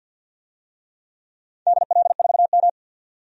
Morse code, heard as a keyed tone, about 46 wpm, 700 Hz. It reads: DL4M
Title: DXCC-100 Call Signs - 4 Characters - 50wpm